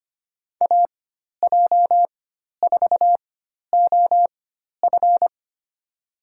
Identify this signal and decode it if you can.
Morse code, heard as a keyed tone, about 25 wpm, spaced as for 12 wpm, 700 Hz. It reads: AJ4OF
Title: USA Call Signs - 5 Characters - 4x Character Spacing - 25wpm